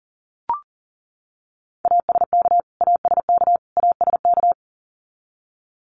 Morse code, heard as a keyed tone, about 40 wpm, 700 Hz. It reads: ASK ASK ASK
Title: Top 500 Words Repeated 40wpm